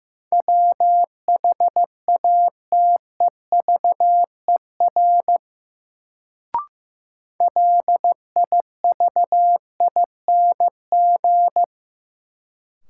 Morse code, heard as a keyed tone, about 16 wpm, 700 Hz. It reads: WHATEVER LIVING
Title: Sentences from Top 1100 Words - New Word Drill Alt - 15wpm